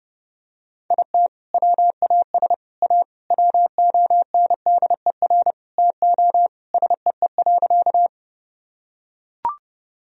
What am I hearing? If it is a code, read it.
Morse code, heard as a keyed tone, about 30 wpm, 700 Hz. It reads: IT WAS A WONDER TO SEE.
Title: Sentences from Top 1000 Words - Review Part 3 - 30wpm